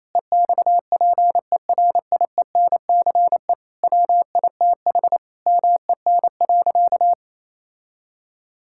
Morse code, heard as a keyed tone, about 28 wpm, 700 Hz. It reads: EXPERIENCE WITH MEN.
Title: Sentences from Top 1000 Words - Review Part 5 - 28wpm